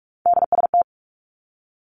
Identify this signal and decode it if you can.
Morse code, heard as a keyed tone, about 46 wpm, 700 Hz. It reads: DST
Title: Three Challenging Letter-Number Combinations 45wpm